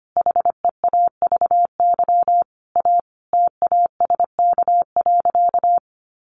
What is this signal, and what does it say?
Morse code, heard as a keyed tone, about 25 wpm, 700 Hz. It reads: HEAVY A TASK.
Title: Sentences from Top 1100 Words 25wpm